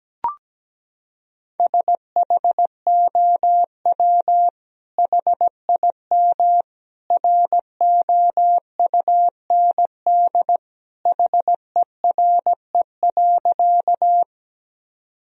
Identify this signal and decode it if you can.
Morse code, heard as a keyed tone, about 17 wpm, 700 Hz. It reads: SHOW HIM ROUND HERE.
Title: Sentences from Top 1000 Words - Review Part 1 - 17wpm